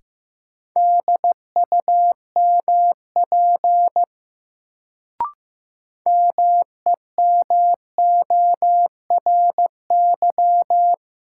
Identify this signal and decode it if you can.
Morse code, heard as a keyed tone, about 16 wpm, 700 Hz. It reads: DUMP MEMORY
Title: Sentences from Top 1300 Words - New Word Drill - 15wpm